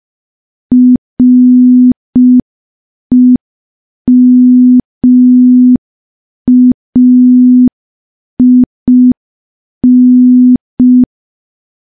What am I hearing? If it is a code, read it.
Morse code, heard as a keyed tone, about 5 wpm, 250 Hz. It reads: REMAIN
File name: morse / morse-training-05WPM